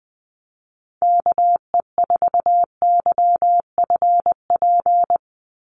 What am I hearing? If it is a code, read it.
Morse code, heard as a keyed tone, about 20 wpm, 700 Hz. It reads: KE4YFP
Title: USA Call Signs 20wpm